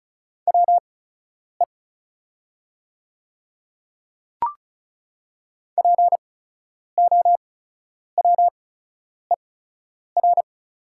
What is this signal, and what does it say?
Morse code, heard as a keyed tone, about 34 wpm, spaced as for 10 wpm, 690 Hz. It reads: WE POWER
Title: 8x Character Spacing - Top 500 Words - 35wpm